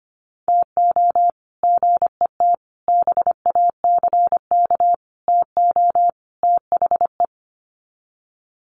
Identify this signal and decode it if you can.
Morse code, heard as a keyed tone, about 25 wpm, 700 Hz. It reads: TO GET BACK TO THE
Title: Sets of 5 Words Encore 25wpm